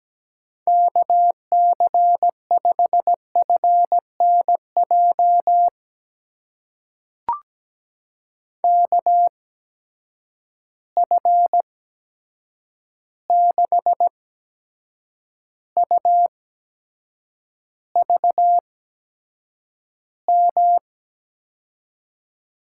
Morse code, heard as a keyed tone, about 17 wpm, spaced as for 7 wpm, 700 Hz. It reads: KC5FNJ KF6UVM
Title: USA Call Signs - 6 Characters - 8x Character Spacing - 17wpm